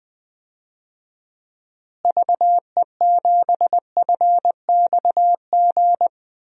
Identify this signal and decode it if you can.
Morse code, heard as a keyed tone, about 20 wpm, 700 Hz. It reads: VE7FXG
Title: DXCC-100 Call Signs - 6 Characters - Repeated 2x - 20wpm